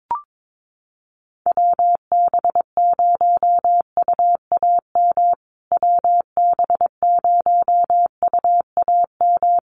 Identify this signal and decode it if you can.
Morse code, heard as a keyed tone, about 22 wpm, 700 Hz. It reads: WB0UAM WB0UAM
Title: USA Call Signs Repeated 2x 22wpm